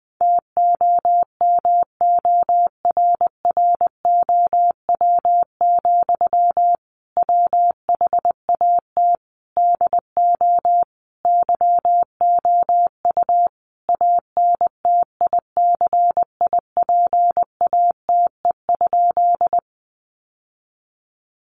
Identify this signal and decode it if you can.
Morse code, heard as a keyed tone, about 20 wpm, 700 Hz. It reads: TOMORROW, WHAT DO YOU ANTICIPATE?
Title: Sentences from Top 1500 Words 20wpm